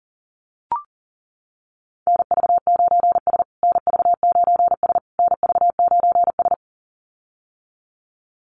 Morse code, heard as a keyed tone, about 40 wpm, 700 Hz. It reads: NV9S NV9S NV9S
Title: USA Call Signs Repeated 3x 40wpm